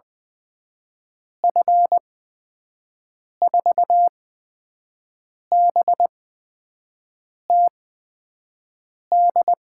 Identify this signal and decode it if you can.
Morse code, heard as a keyed tone, about 20 wpm, spaced as for 5 wpm, 700 Hz. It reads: F4BTD
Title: USA Call Signs - 8x Character Spacing - 20wpm